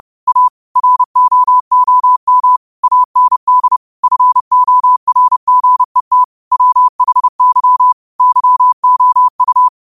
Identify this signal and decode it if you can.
Morse code, heard as a keyed tone, about 30 wpm, 1000 Hz. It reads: A ROOM AND FORGET WHY YOU
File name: qotdfiles / 30wpm